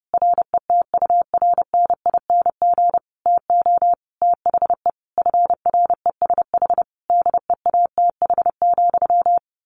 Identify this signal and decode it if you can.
Morse code, heard as a keyed tone, about 30 wpm, 700 Hz. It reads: RETURNING TO THE FRESH DEATH,
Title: A Sound Of Thunder 30wpm